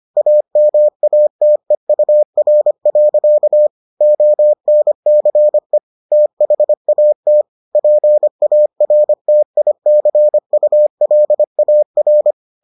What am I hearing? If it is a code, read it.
Morse code, heard as a keyed tone, about 25 wpm, 600 Hz. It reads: AMATEUR. ONCE THAT PARTICULAR